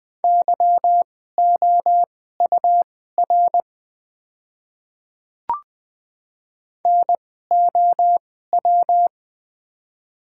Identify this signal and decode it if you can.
Morse code, heard as a keyed tone, about 20 wpm, spaced as for 15 wpm, 700 Hz. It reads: YOUR NOW
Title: 2x Character Spacing - Top 300 Words - 20wpm